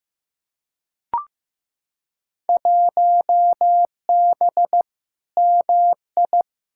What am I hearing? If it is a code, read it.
Morse code, heard as a keyed tone, about 15 wpm, 700 Hz. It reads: 1B MI
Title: Field Day Exchanges - 15wpm